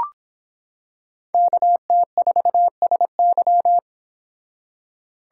Morse code, heard as a keyed tone, about 26 wpm, 700 Hz. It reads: KT4SY